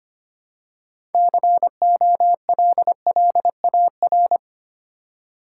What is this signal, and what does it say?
Morse code, heard as a keyed tone, about 25 wpm, 690 Hz. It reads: COLLAR
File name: practice / 25wpm / Easy Words with Hints - 25wpm